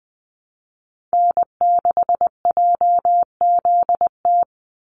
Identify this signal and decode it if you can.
Morse code, heard as a keyed tone, about 20 wpm, 700 Hz. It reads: N6JZT
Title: NA SOTA Activator Callsigns 20wpm